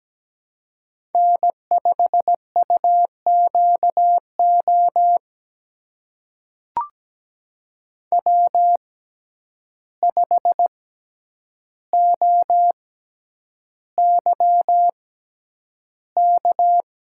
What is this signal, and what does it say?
Morse code, heard as a keyed tone, about 17 wpm, spaced as for 9 wpm, 700 Hz. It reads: N5UQO W5OYK
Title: USA Call Signs - 5 Characters - 6x Character Spacing - 17wpm